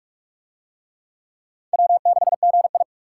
Morse code, heard as a keyed tone, about 44 wpm, 700 Hz. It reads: W6GI